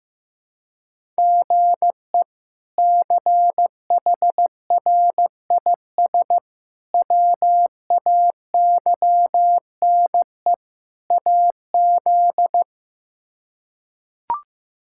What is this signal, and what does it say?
Morse code, heard as a keyed tone, about 16 wpm, 700 Hz. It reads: GE CHRIS WAYNE AZ
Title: K1USN SST QSO Practice - 15wpm